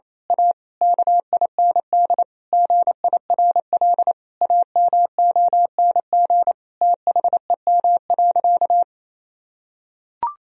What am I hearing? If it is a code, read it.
Morse code, heard as a keyed tone, about 28 wpm, 700 Hz. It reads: A KIND GIRL AMONG THEM.
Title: Sentences from Top 1000 Words - Review Part 3 - 28wpm